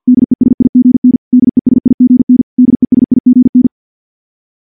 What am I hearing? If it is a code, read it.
Morse code, heard as a keyed tone, about 50 wpm, 250 Hz. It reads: DESIGN DESIGN DESIGN
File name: morse / morse-training-50WPM